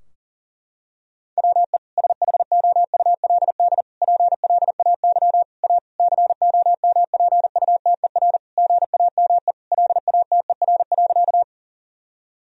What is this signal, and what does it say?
Morse code, heard as a keyed tone, about 40 wpm, 700 Hz. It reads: WE SHOULD PLAY A COMPUTER GAME LATER.
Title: Sentences from Top 1000 Words - Review Part 3 - 40wpm